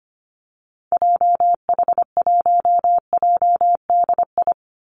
Morse code, heard as a keyed tone, about 25 wpm, 700 Hz. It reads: JH1JDI
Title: Call Signs - 25wpm